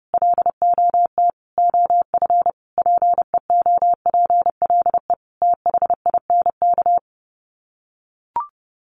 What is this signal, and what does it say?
Morse code, heard as a keyed tone, about 30 wpm, 700 Hz. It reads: LOT OF PEOPLE THINK
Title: Sets of 5 Words Encore 30wpm